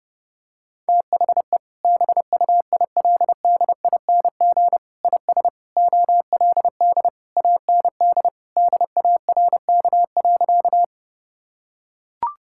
Morse code, heard as a keyed tone, about 30 wpm, 700 Hz. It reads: THE BUILDING IS OLD AND DARK.
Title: Sentences from Top 1000 Words - Review Part 5 - 30wpm